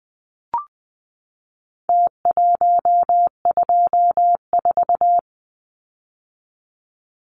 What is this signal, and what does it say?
Morse code, heard as a keyed tone, about 20 wpm, 700 Hz. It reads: T124
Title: Numbers - Contest Style - 4-Digit 20wpm